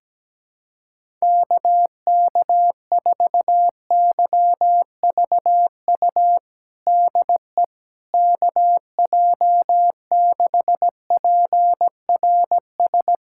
Morse code, heard as a keyed tone, about 17 wpm, 700 Hz. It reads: KK4YVU DE KJ6PRS
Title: USA Callsign DE Callsign 17wpm